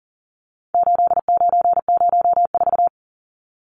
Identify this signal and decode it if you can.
Morse code, heard as a keyed tone, about 40 wpm, 700 Hz. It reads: 8904
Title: Numbers - 4-Digit 40wpm